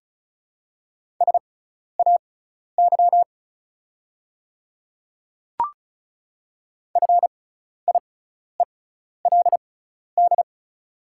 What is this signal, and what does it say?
Morse code, heard as a keyed tone, about 36 wpm, spaced as for 12 wpm, 700 Hz. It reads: SAY FIELD
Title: 6x Character Spacing - Top 500 Words - 35wpm